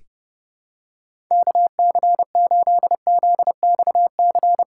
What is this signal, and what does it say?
Morse code, heard as a keyed tone, about 30 wpm, 700 Hz. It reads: KC8ZXC
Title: USA Call Signs - 6 Characters - 3x Character Spacing - 30wpm